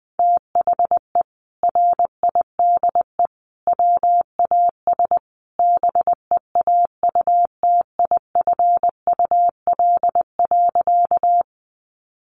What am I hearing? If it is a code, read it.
Morse code, heard as a keyed tone, about 20 wpm, 700 Hz. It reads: THE RIDE WAS BEAUTIFUL.
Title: Sentences from Top 1200 Words 20wpm